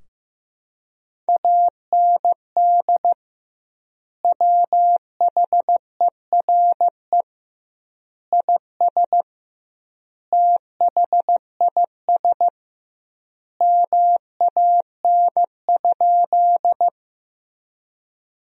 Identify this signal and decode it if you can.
Morse code, heard as a keyed tone, about 15 wpm, 700 Hz. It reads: AND WHERE IS THIS MAN?
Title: 2x Word-Spacing - Sentences from Top 200 Words - 15wpm